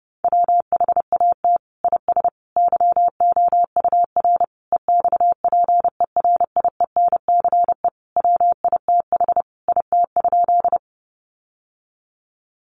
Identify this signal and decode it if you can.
Morse code, heard as a keyed tone, about 30 wpm, 700 Hz. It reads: WHAT IS YOUR EXPERIENCE WITH IT?
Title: Sentences from Top 600 Words 30wpm